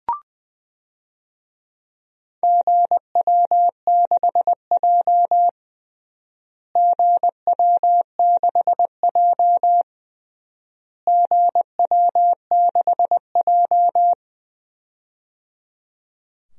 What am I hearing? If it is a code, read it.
Morse code, heard as a keyed tone, about 20 wpm, 700 Hz. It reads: GW6J GW6J GW6J
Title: DXCC-100 Call Signs - 4 Characters - Repeated 3x - 20wpm